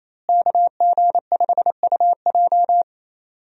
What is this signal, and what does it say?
Morse code, heard as a keyed tone, about 28 wpm, 700 Hz. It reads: KG5UJ
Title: USA Call Signs - 5 Characters - Repeated 3x - 28wpm